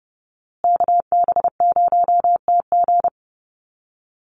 Morse code, heard as a keyed tone, about 30 wpm, 700 Hz. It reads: KB0TG
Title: USA Call Signs Repeated 3x 30wpm